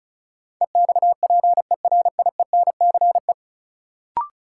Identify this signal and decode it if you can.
Morse code, heard as a keyed tone, about 34 wpm, 700 Hz. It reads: EXPERIENCE
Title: Sentences from Top 600 Words - New Word Drill - 35wpm